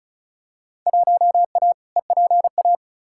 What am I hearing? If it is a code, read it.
Morse code, heard as a keyed tone, about 34 wpm, 700 Hz. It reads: 1A EPA
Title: Field Day Exchanges - 35wpm